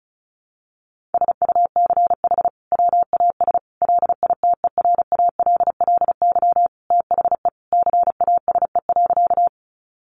Morse code, heard as a keyed tone, about 34 wpm, 700 Hz. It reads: SUCH WAS LITERALLY THE CASE.
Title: Sentences from Top 1400 Words 35wpm